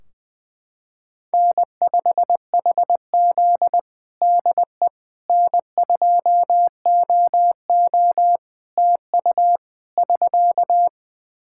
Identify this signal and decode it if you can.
Morse code, heard as a keyed tone, about 20 wpm, 700 Hz. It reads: N5HZ DE N2OO TU <SK>
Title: Practice QSO - Exchange 4 - Ending - 20wpm